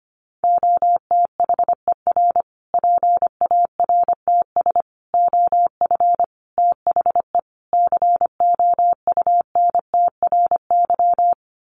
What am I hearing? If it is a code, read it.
Morse code, heard as a keyed tone, about 25 wpm, 700 Hz. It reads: OTHER PARTS OF THE COUNTRY
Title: Sets of 5 Words Encore 25wpm